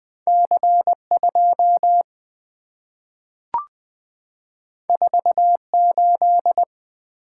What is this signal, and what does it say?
Morse code, heard as a keyed tone, about 20 wpm, 690 Hz. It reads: C2 48
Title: Two Letter-Number Combinations - 20wpm